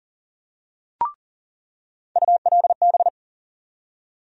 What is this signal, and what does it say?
Morse code, heard as a keyed tone, about 40 wpm, 700 Hz. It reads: ULB
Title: USA Call Sign Postfixes - 3 Letters 40wpm